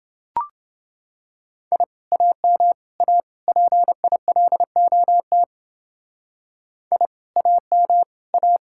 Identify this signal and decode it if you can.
Morse code, heard as a keyed tone, about 29 wpm, 700 Hz. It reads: I AM A PILOT I AM A